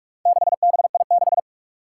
Morse code, heard as a keyed tone, about 46 wpm, 700 Hz. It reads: 6BI6
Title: Four Challenging Letter-Number Combinations 45wpm